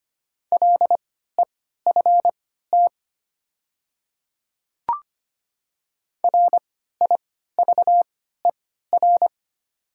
Morse code, heard as a keyed tone, about 25 wpm, spaced as for 14 wpm, 700 Hz. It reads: LEFT RIVER
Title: 3x Character Spacing - Top 500 Words - 25wpm